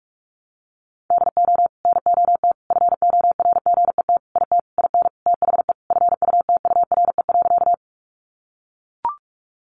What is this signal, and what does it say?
Morse code, heard as a keyed tone, about 44 wpm, 700 Hz. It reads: DO NOT FORGET IT IN THE FUTURE.
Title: Sentences from Top 1200 Words 45wpm